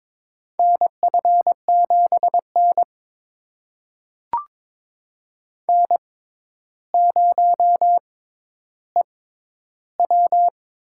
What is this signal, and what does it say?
Morse code, heard as a keyed tone, about 23 wpm, spaced as for 11 wpm, 700 Hz. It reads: NF7N N0EW
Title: USA Call Signs - 6x Character Spacing - 22wpm